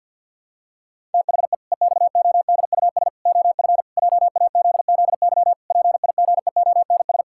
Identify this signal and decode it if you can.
Morse code, heard as a keyed tone, about 52 wpm, 700 Hz. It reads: THE EXODUS OF JAZZY PIGEONS
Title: Pangram Challenge v2 - 50wpm